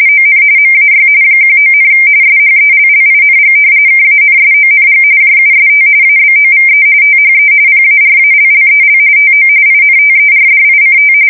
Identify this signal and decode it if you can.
RTTY, about 45.45 baud. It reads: ON MAY 19, SHELLING KANSONG JUST ABOVE THE PARALLEL TO INTERDICT
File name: fsk170 / battleshipnj